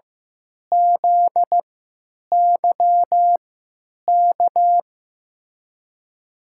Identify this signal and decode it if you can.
Morse code, heard as a keyed tone, about 15 wpm, spaced as for 9 wpm, 700 Hz. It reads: ZYK
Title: USA Call Signs - 6 Characters - 3x Character Spacing - 15wpm